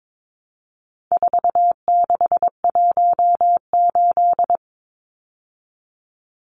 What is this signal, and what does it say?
Morse code, heard as a keyed tone, about 22 wpm, 700 Hz. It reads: 4618
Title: Numbers - 4-Digit 22wpm